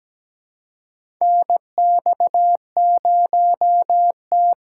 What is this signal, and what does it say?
Morse code, heard as a keyed tone, about 17 wpm, 700 Hz. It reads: NX0T
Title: USA Call Signs - 4 Characters - 3x Character Spacing - 17wpm